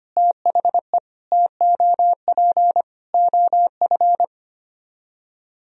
Morse code, heard as a keyed tone, about 30 wpm, 700 Hz. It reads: THE TOP OF
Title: Speed-Racing - Sets of 4 Words 25wpm